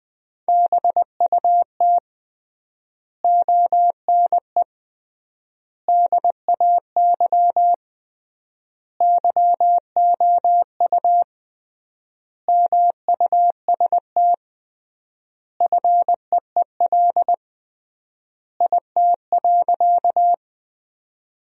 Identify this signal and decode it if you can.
Morse code, heard as a keyed tone, about 20 wpm, 700 Hz. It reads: BUT ONE DAY YOU MUST FEEL IT.
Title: 3x Word-Spacing - Sentences from Top 400 Words - 20wpm